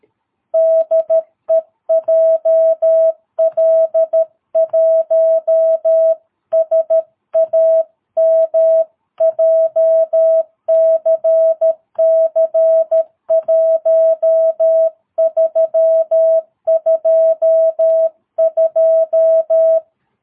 Morse code, heard as a keyed tone, about 13 wpm, 660 Hz. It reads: DE JL1SAM JCC1322
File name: jh8cbh / 2022cw